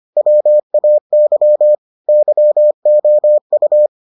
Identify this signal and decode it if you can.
Morse code, heard as a keyed tone, about 25 wpm, 600 Hz. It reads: WAY YOU